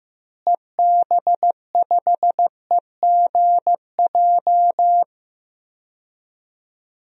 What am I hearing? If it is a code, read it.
Morse code, heard as a keyed tone, about 15 wpm, 700 Hz. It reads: EB5EGJ
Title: DXCC-100 Call Signs - 6 Characters - Repeated 2x - 15wpm